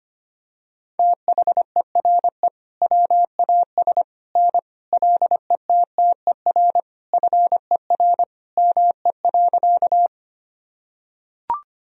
Morse code, heard as a keyed tone, about 25 wpm, 700 Hz. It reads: THERE WAS N LETTER FER ME.
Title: Sentences from Top 300 Words and Abbreviations 25wpm